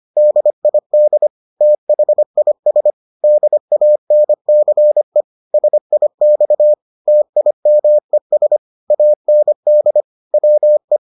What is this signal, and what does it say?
Morse code, heard as a keyed tone, about 25 wpm, 600 Hz. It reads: DID THIS DANCE SIX TIMES AND WE